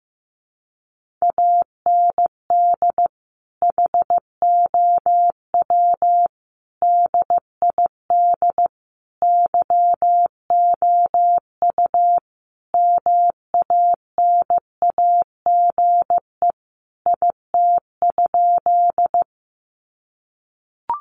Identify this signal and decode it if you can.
Morse code, heard as a keyed tone, about 15 wpm, 700 Hz. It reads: AND HOW DID YOU MANAGE IT?
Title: Sentences from Top 1400 Words 15wpm